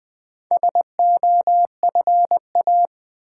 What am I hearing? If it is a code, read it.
Morse code, heard as a keyed tone, about 20 wpm, 700 Hz. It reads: SOFA
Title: Easy Words with Hints - 20wpm